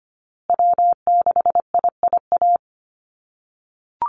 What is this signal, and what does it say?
Morse code, heard as a keyed tone, about 25 wpm, 700 Hz. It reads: W6IIA
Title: USA Call Signs Repeated 2x 25wpm